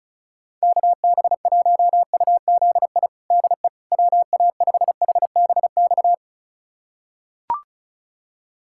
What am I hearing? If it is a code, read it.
Morse code, heard as a keyed tone, about 36 wpm, 700 Hz. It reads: KB1UZI DE WA5HBX
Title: USA Callsign DE Callsign 35wpm